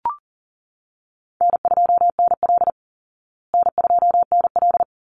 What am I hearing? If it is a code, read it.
Morse code, heard as a keyed tone, about 40 wpm, 700 Hz. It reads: N2NL N2NL
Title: Easy Call Signs Repeated 2x - 40wpm